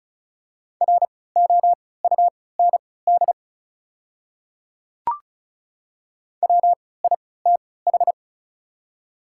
Morse code, heard as a keyed tone, about 36 wpm, spaced as for 20 wpm, 700 Hz. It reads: ROUND WITH
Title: 3x Character Spacing - Top 200 Words - 35wpm